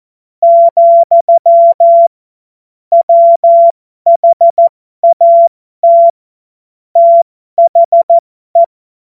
Morse code, heard as a keyed tone, about 14 wpm, spaced as for 12 wpm, 680 Hz. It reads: , WHAT THE